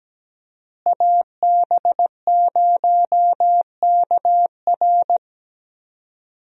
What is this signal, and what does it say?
Morse code, heard as a keyed tone, about 17 wpm, 700 Hz. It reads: AB0KR
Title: USA Call Signs - 5 Characters - Repeated 2x - 17wpm